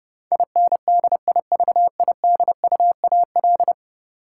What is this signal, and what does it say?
Morse code, heard as a keyed tone, about 30 wpm, 700 Hz. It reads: INDIVIDUAL
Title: Sentences from Top 1000 Words - Review Part 4 - New Word Drill - 30wpm